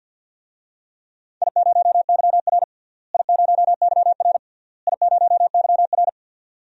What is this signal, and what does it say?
Morse code, heard as a keyed tone, about 52 wpm, 700 Hz. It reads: I0YR I0YR I0YR
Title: DXCC-100 Call Signs - 4 Characters - Repeated 3x - 50wpm